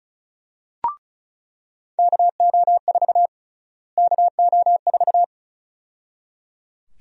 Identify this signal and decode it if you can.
Morse code, heard as a keyed tone, about 34 wpm, 700 Hz. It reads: KO4 KO4
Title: USA Call Sign Prefixes - Repeated 2x - 35wpm